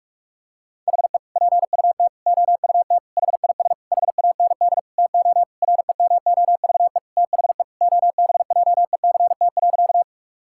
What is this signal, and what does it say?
Morse code, heard as a keyed tone, about 46 wpm, 690 Hz. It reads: HE PUT OUT HIS HAND TO REMOVE THE OBJECT.
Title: Sentences from Top 1000 Words - Review Part 5 - 45wpm